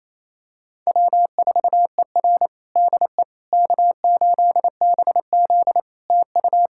Morse code, heard as a keyed tone, about 28 wpm, 700 Hz. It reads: W4ER DE K8BZ TU
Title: Practice QSO - Exchange 4 - Ending - 28wpm